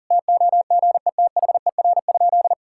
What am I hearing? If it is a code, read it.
Morse code, heard as a keyed tone, about 37 wpm, 700 Hz. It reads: TOGETHER?
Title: Sentences from Top 1600 Words 40wpm